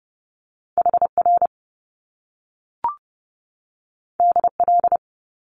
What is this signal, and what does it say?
Morse code, heard as a keyed tone, about 30 wpm, 700 Hz. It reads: HR DL
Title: Top Two Letter Combinations in Words 30wpm